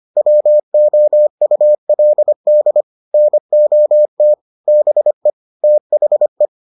Morse code, heard as a keyed tone, about 25 wpm, 600 Hz. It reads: WOULD NOT BE THE